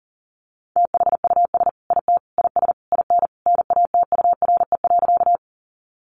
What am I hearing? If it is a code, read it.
Morse code, heard as a keyed tone, about 40 wpm, 700 Hz. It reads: THUS IT IS IN NATURE.
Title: Sentences from Top 800 Words 40wpm